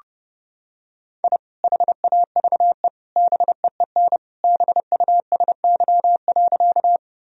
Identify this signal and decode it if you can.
Morse code, heard as a keyed tone, about 30 wpm, 700 Hz. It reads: I HAVE BEEN BUSY.
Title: Sentences from Top 1000 Words - Review Part 3 - 30wpm